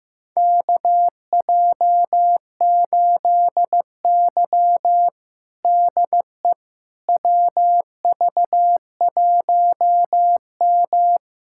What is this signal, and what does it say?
Morse code, heard as a keyed tone, about 15 wpm, 700 Hz. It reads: KJ8Y DE WV1M
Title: USA Callsign DE Callsign - 4 Characters - 15wpm